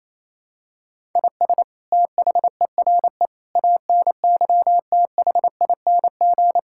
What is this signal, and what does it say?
Morse code, heard as a keyed tone, about 28 wpm, 700 Hz. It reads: IS THERE ANYTHING
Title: Sentences from Top 1000 Words - Review Part 6 - 28wpm